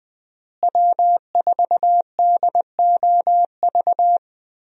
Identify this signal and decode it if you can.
Morse code, heard as a keyed tone, about 20 wpm, 700 Hz. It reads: W4DOV
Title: USA Call Signs - 5 Characters - Repeated 3x - 20wpm